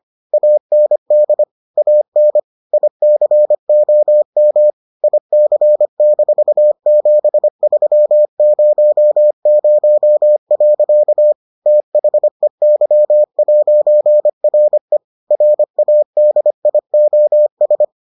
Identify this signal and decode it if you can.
Morse code, heard as a keyed tone, about 25 wpm, 600 Hz. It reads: AND AN ICOM IC-7300. THEY'RE RADIOS